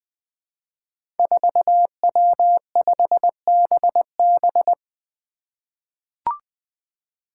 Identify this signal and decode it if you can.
Morse code, heard as a keyed tone, about 20 wpm, 700 Hz. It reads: 4W5BB
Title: Five Challenging Letter-Number Combinations 20wpm